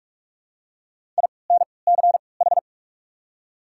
Morse code, heard as a keyed tone, about 42 wpm, spaced as for 25 wpm, 690 Hz. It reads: INCH
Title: 3x Character Spacing - Top 500 Words - 45wpm